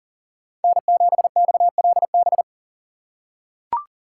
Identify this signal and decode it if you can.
Morse code, heard as a keyed tone, about 38 wpm, 700 Hz. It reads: N7XLB
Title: USA Call Signs - 5 Characters - 2x Character Spacing - 40wpm